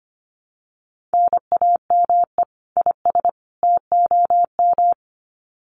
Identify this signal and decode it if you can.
Morse code, heard as a keyed tone, about 25 wpm, 700 Hz. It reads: NAME IS TOM
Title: Common QSO Phrases 25wpm